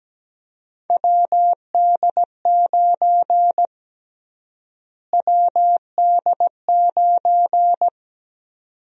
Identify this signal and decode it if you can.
Morse code, heard as a keyed tone, about 17 wpm, 700 Hz. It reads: WD9 WD9
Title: USA Call Sign Prefixes - Repeated 2x - 17wpm